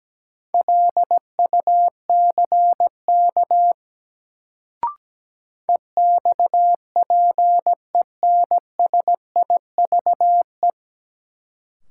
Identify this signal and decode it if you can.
Morse code, heard as a keyed tone, about 17 wpm, 700 Hz. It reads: LUCK EXPENSIVE
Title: Sentences from Top 1500 Words - New Word Drill Alt - 17wpm